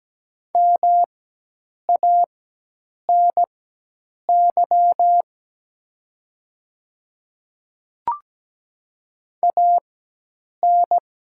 Morse code, heard as a keyed tone, about 17 wpm, spaced as for 8 wpm, 700 Hz. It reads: MANY AN
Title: 4x Character Spacing - Top 100 Words - 17wpm